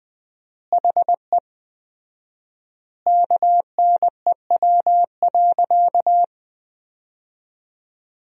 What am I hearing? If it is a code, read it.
Morse code, heard as a keyed tone, about 20 wpm, 700 Hz. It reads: HE KNEW.
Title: 4x Word-Spacing - Sentences from Top 400 Words - 20wpm